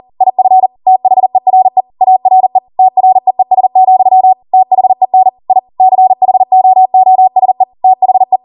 Morse code, heard as a keyed tone, about 40 wpm, 750 Hz. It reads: IF THERE ARE TREES, THEN I CHOOSE THE